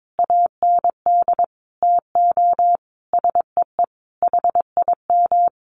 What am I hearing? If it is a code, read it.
Morse code, heard as a keyed tone, about 22 wpm, 700 Hz. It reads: AND TO SEE HIM
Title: Sentences from Top 1400 Words 22wpm